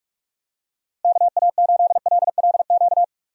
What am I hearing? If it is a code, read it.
Morse code, heard as a keyed tone, about 46 wpm, 700 Hz. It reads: KA8LLQ
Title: USA Call Signs - 6 Characters - Repeated 3x - 45wpm